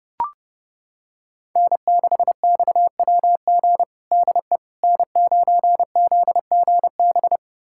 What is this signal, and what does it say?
Morse code, heard as a keyed tone, about 30 wpm, 700 Hz. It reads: N6XWG DE N9ZGB
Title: USA Callsign DE Callsign - 5 Characters - 30wpm